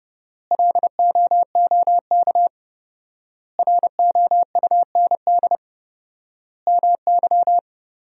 Morse code, heard as a keyed tone, about 30 wpm, 700 Hz. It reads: LOOK ROUND MY
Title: 4x Word-Spacing - Sentences from Top 200 Words - 30wpm